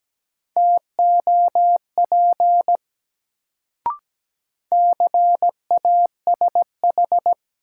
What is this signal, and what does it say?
Morse code, heard as a keyed tone, about 17 wpm, 700 Hz. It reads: TOP CASH
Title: Sentences from Top 700 Words - New Word Drill - 17wpm